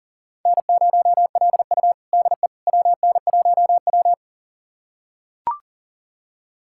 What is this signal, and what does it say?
Morse code, heard as a keyed tone, about 37 wpm, 700 Hz. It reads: N0LU DE WN1W
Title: USA Callsign DE Callsign - 4 Characters - 40wpm